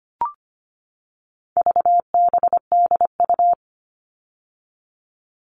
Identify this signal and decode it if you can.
Morse code, heard as a keyed tone, about 25 wpm, 700 Hz. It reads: VBDU
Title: All Cut Numbers - 4-Digit 25wpm